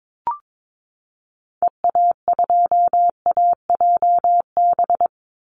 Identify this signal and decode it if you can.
Morse code, heard as a keyed tone, about 22 wpm, 700 Hz. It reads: EA2AJB
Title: Call Signs Repeated 2x - 22wpm